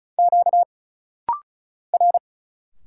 Morse code, heard as a keyed tone, about 36 wpm, 700 Hz. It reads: Q R
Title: Single Letter-Number Alt - Rapid-Fire 35wpm